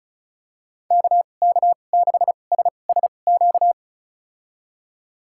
Morse code, heard as a keyed tone, about 36 wpm, spaced as for 25 wpm, 700 Hz. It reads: KK6SSQ
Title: USA Call Signs - 6 Characters - 2x Character Spacing - 35wpm